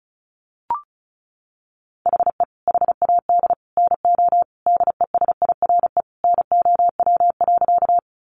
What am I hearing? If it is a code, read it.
Morse code, heard as a keyed tone, about 34 wpm, 700 Hz. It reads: HE HAD NO DESIRE NOW.
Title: Sentences from Top 1100 Words 35wpm